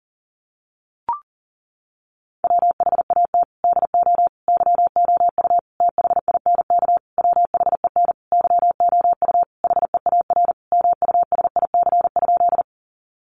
Morse code, heard as a keyed tone, about 40 wpm, 700 Hz. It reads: WHAT DO YOU THINK WHEN YOU HEAR MUSIC?
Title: Sentences from Top 300 Words 40wpm